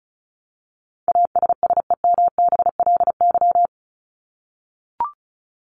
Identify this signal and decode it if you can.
Morse code, heard as a keyed tone, about 34 wpm, 700 Hz. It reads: ASSEMBLY
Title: LY Suffix - Top 100 Words with Base Words 35wpm